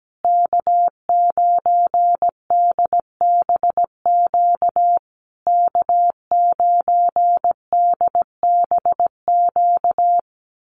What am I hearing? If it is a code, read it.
Morse code, heard as a keyed tone, about 17 wpm, 700 Hz. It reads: K9DBQ K9DBQ
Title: USA Call Signs Repeated 3x 17wpm